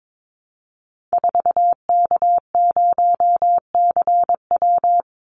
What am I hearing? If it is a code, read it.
Morse code, heard as a keyed tone, about 22 wpm, 690 Hz. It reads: 4K0CW
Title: Call Signs Repeated 3x - 22wpm